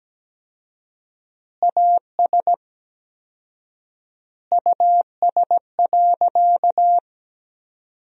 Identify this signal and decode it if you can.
Morse code, heard as a keyed tone, about 17 wpm, 700 Hz. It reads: AS US.
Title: 4x Word-Spacing - Sentences from Top 500 Words - 17wpm